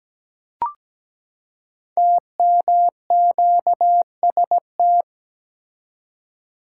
Morse code, heard as a keyed tone, about 17 wpm, 700 Hz. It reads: TMQST
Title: Code Groups - 5 Letter:Numbers - 17wpm